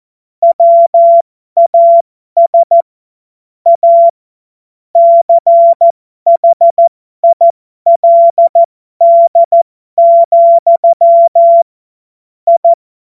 Morse code, heard as a keyed tone, about 14 wpm, spaced as for 12 wpm, 680 Hz. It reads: WAS A CHILD, I
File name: Morsekurs / Witze